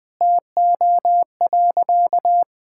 Morse code, heard as a keyed tone, about 20 wpm, 700 Hz. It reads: TO.
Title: Sentences from Top 1600 Words 20wpm